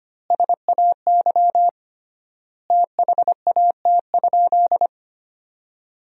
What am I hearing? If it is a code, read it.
Morse code, heard as a keyed tone, about 25 wpm, 700 Hz. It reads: SAY THAT?
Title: 3x Word-Spacing - Sentences from Top 200 Words - 25wpm